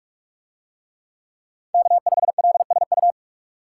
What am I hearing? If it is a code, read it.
Morse code, heard as a keyed tone, about 45 wpm, 700 Hz. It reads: K5LSU